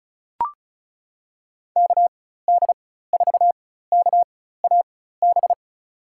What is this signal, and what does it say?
Morse code, heard as a keyed tone, about 36 wpm, spaced as for 16 wpm, 700 Hz. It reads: KD4KAB
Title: USA Call Signs - 6 Characters - 4x Character Spacing - 35wpm